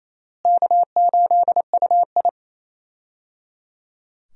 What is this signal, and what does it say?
Morse code, heard as a keyed tone, about 28 wpm, 700 Hz. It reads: K8UI
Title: USA Call Signs - 4 Characters - Repeated 3x - 28wpm